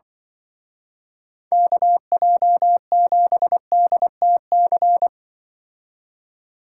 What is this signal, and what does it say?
Morse code, heard as a keyed tone, about 23 wpm, 700 Hz. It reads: KJ7DTC
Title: Speed Racing - USA 2x3 Call Signs 20wpm